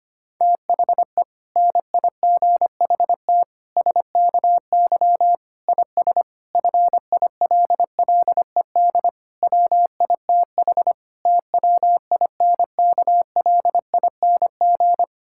Morse code, heard as a keyed tone, about 25 wpm, 700 Hz. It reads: THE NIGHT SKY IS FILLED WITH TWINKLING
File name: practice / 25wpm / Difficult Sentences with Hints - 25wpm